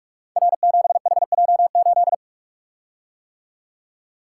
Morse code, heard as a keyed tone, about 44 wpm, 700 Hz. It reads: R7HJ8
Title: Code Groups - 5 Letter:Numbers - 45wpm